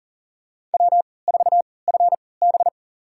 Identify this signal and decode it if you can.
Morse code, heard as a keyed tone, about 40 wpm, spaced as for 23 wpm, 700 Hz. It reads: W4FB
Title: USA Call Signs - 3x Character Spacing - 40wpm